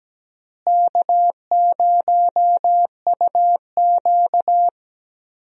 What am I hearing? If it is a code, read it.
Morse code, heard as a keyed tone, about 17 wpm, 700 Hz. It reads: K0UQ
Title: USA Call Signs - 4 Characters - Repeated 3x - 17wpm